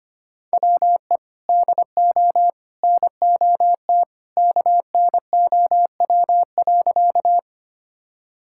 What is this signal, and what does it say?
Morse code, heard as a keyed tone, about 25 wpm, 700 Hz. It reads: WE DO NOT KNOW.
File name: practice / 25wpm / Sentences with 3-Word Phrases - 25wpm